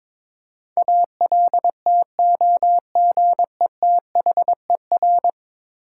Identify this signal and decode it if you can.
Morse code, heard as a keyed tone, about 21 wpm, 700 Hz. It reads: ALTOGETHER
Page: morse-code-ninja-practice-22wpm